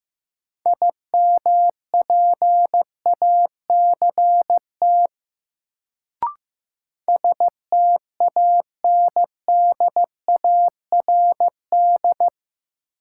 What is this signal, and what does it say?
Morse code, heard as a keyed tone, about 15 wpm, 700 Hz. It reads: IMPACT STANDARD
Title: Sentences from Top 1000 Words - Review Part 6 - New Word Drill - 15wpm